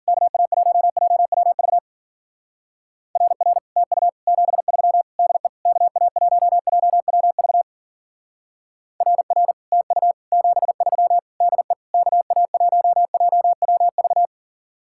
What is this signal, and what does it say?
Morse code, heard as a keyed tone, about 50 wpm, 700 Hz. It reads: KA1JWV RR TU 73 DE KA1JWV RR TU 73 DE KA1JWV
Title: Speed-Racing Common QSO Phrases Encore - 40wpm